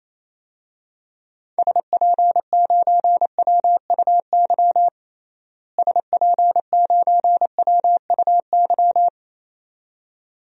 Morse code, heard as a keyed tone, about 28 wpm, 700 Hz. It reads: SP9WUY SP9WUY
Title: DXCC-100 Call Signs - 6 Characters - Repeated 2x - 28wpm